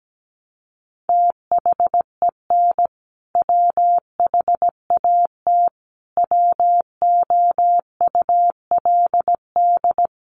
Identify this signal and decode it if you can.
Morse code, heard as a keyed tone, about 17 wpm, 700 Hz. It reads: THEN WHAT WOULD
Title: Sentences from Top 400 Words 17wpm